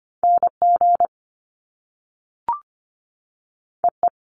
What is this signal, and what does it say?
Morse code, heard as a keyed tone, about 25 wpm, 700 Hz. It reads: NG EE
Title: Top Two Letter Combinations in Words 25wpm